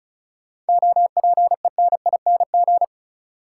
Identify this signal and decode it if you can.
Morse code, heard as a keyed tone, about 36 wpm, 700 Hz. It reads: OPENING
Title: Seven Letter Words - 35wpm